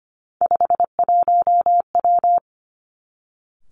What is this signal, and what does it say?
Morse code, heard as a keyed tone, about 25 wpm, 700 Hz. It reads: 51W
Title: Three Challenging Letter-Number Combinations 25wpm